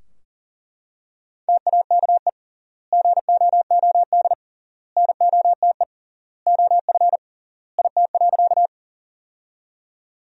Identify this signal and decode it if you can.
Morse code, heard as a keyed tone, about 40 wpm, 700 Hz. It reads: TAKE GOOD NOTE OF IT.
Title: 3x Word-Spacing - Sentences from Top 500 Words - 40wpm